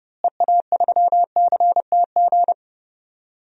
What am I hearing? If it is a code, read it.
Morse code, heard as a keyed tone, about 30 wpm, 700 Hz. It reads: EA3CTG
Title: DXCC-100 Call Signs - 6 Characters - Repeated 3x - 30wpm